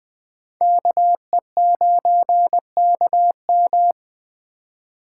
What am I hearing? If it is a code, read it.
Morse code, heard as a keyed tone, about 21 wpm, 700 Hz. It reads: KE9KM